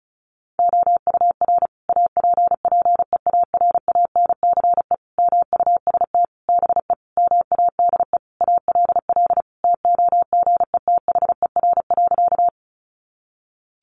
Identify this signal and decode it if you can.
Morse code, heard as a keyed tone, about 34 wpm, 700 Hz. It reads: OUR APPEARANCE MUST BE MADE ALL TOGETHER.
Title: Sentences from Top 1100 Words 35wpm